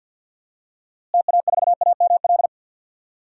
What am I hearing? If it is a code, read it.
Morse code, heard as a keyed tone, about 51 wpm, 700 Hz. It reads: TA4AML
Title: DXCC-100 Call Signs - 6 Characters - Repeated 3x - 50wpm